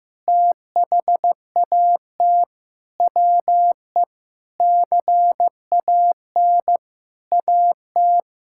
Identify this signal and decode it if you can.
Morse code, heard as a keyed tone, about 14 wpm, 700 Hz. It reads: THAT WE CAN AT
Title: Sentences from Top 1000 Words - Review Part 5 - 15wpm